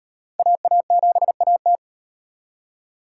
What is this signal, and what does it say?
Morse code, heard as a keyed tone, about 37 wpm, 700 Hz. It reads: AA7AT